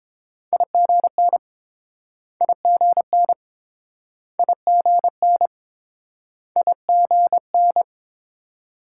Morse code, heard as a keyed tone, about 31 wpm, 700 Hz. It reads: IGN IGN IGN IGN